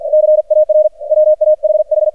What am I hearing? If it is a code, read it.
Morse code, heard as a keyed tone, about 48 wpm, 610 Hz. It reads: CAR WAHH
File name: archived / CRSnet